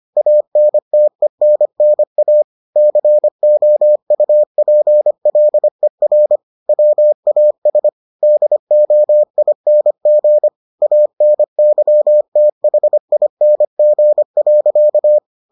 Morse code, heard as a keyed tone, about 25 wpm, 600 Hz. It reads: ANTENNA COUPLER WAS DOING ANYTHING.